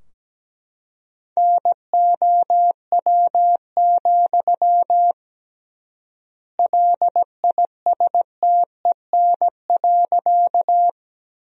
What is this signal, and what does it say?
Morse code, heard as a keyed tone, about 17 wpm, 700 Hz. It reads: NOW, LISTEN.
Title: 3x Word-Spacing - Sentences from Top 400 Words - 17wpm